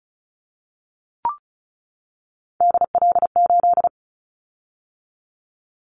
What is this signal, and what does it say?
Morse code, heard as a keyed tone, about 36 wpm, 700 Hz. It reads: DL8
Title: Callsign Prefixes for Top 25 DXCC - 35wpm